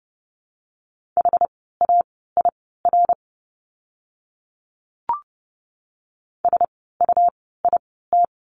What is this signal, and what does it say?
Morse code, heard as a keyed tone, about 30 wpm, spaced as for 17 wpm, 700 Hz. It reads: HAIR SUIT
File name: Lesson 17f - Letter u - Words 30wpm